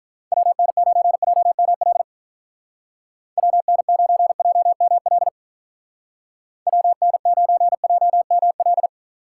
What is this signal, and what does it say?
Morse code, heard as a keyed tone, about 45 wpm, 700 Hz. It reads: WN9JML WN9JML WN9JML